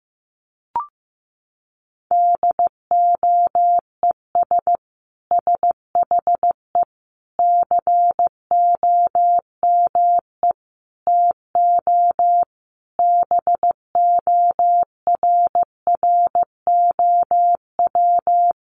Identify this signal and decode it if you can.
Morse code, heard as a keyed tone, about 15 wpm, 700 Hz. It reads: DOES SHE COME TO BORROW
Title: Sentences from Top 1500 Words 15wpm